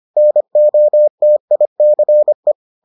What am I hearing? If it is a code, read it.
Morse code, heard as a keyed tone, about 25 wpm, 600 Hz. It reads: NOTICE